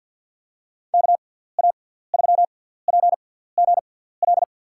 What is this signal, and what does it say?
Morse code, heard as a keyed tone, about 53 wpm, spaced as for 17 wpm, 700 Hz. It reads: KA3PGL
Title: USA Call Signs - 6 Characters - 6x Character Spacing - 50wpm